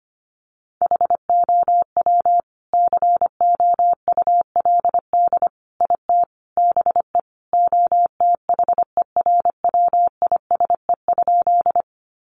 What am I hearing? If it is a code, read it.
Morse code, heard as a keyed tone, about 25 wpm, 700 Hz. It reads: HOW COULD IT BE OTHERWISE?
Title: Sentences from Top 1200 Words 25wpm